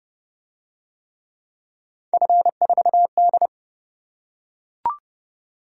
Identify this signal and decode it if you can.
Morse code, heard as a keyed tone, about 30 wpm, 700 Hz. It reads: F4D